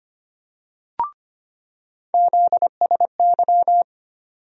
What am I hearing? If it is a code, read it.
Morse code, heard as a keyed tone, about 25 wpm, 700 Hz. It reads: ZSY
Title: USA Call Sign Postfixes - 3 Letters 25wpm